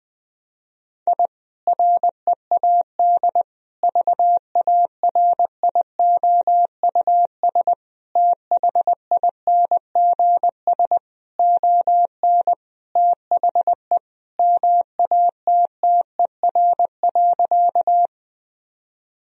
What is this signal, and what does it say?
Morse code, heard as a keyed tone, about 19 wpm, 700 Hz. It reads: I READ VARIOUS THINGS ON THE MATTER.
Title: Sentences from Top 1000 Words - Review Part 4 - 20wpm